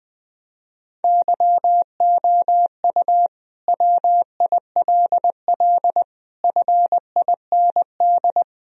Morse code, heard as a keyed tone, about 21 wpm, 700 Hz. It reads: YOU WILL FIND
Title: Sentences from Top 1000 Words - Review Part 4 - 20wpm